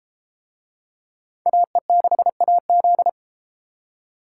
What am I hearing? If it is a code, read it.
Morse code, heard as a keyed tone, about 32 wpm, 700 Hz. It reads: AE6AZ